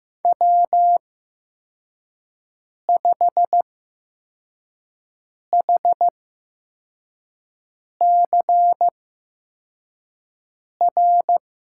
Morse code, heard as a keyed tone, about 15 wpm, spaced as for 4 wpm, 700 Hz. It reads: W5HCR
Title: USA Call Signs - 5 Characters - 8x Character Spacing - 15wpm